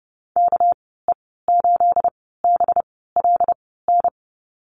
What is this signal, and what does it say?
Morse code, heard as a keyed tone, about 30 wpm, spaced as for 17 wpm, 700 Hz. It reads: KE8BLN
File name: Lesson 37g - Letter x - Call Signs 30wpm